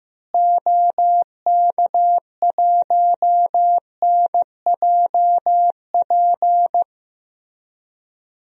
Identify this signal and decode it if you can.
Morse code, heard as a keyed tone, about 15 wpm, 700 Hz. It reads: OK1NJP
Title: DXCC-100 Call Signs - 6 Characters - Repeated 3x - 15wpm